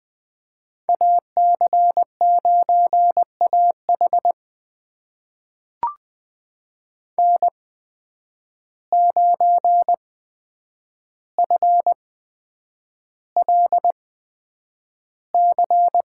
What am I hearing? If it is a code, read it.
Morse code, heard as a keyed tone, about 20 wpm, spaced as for 9 wpm, 700 Hz. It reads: AC9AH N9FLC
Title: USA Call Signs - 5 Characters - 8x Character Spacing - 20wpm